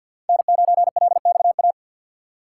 Morse code, heard as a keyed tone, about 52 wpm, 700 Hz. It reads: N9LXA
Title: USA Call Signs - 5 Characters - 2x Character Spacing - 50wpm